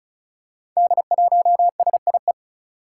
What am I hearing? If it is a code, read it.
Morse code, heard as a keyed tone, about 36 wpm, 700 Hz. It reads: D1SIE
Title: Five Challenging Letter-Number Combinations 35wpm